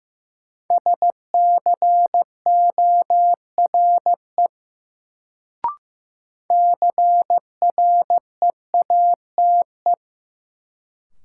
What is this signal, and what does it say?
Morse code, heard as a keyed tone, about 15 wpm, 700 Hz. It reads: SCORE CREATE
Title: Sentences from Top 1600 Words - New Word Drill Alt - 15wpm